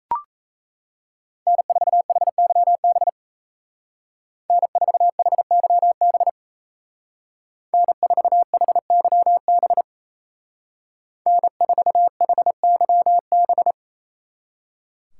Morse code, heard as a keyed tone, about 34 wpm, 700 Hz. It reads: N4HYB N4HYB N4HYB N4HYB
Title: Speed Racing - USA 1x3 Call Signs 28wpm